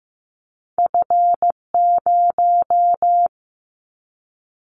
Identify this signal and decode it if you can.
Morse code, heard as a keyed tone, about 15 wpm, 700 Hz. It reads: F0
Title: Callsign Prefixes for Top 50 DXCC - 15wpm